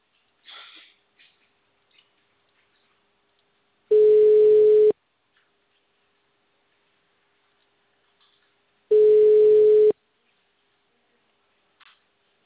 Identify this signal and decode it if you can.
Ringback tone (Europe, 425 Hz).
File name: out- -2002- - - .38159